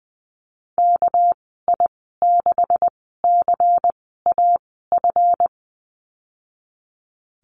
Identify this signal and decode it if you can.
Morse code, heard as a keyed tone, about 20 wpm, spaced as for 15 wpm, 700 Hz. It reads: KI6CAF
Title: 213 Lesson 34g - Number 8 - Call Signs 20wpm